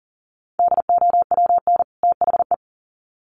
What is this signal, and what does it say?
Morse code, heard as a keyed tone, about 40 wpm, 700 Hz. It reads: DOWN THE
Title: Sets of 2 Words 40wpm